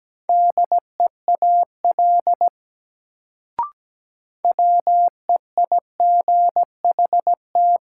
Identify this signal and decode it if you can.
Morse code, heard as a keyed tone, about 17 wpm, 700 Hz. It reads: DEAL WEIGHT
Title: Sentences from Top 800 Words - New Word Drill - 17wpm